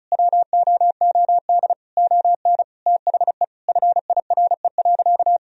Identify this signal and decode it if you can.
Morse code, heard as a keyed tone, about 34 wpm, 700 Hz. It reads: WOOD ON THE FIRE.
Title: Sentences from Top 1000 Words - Review Part 2 - 35wpm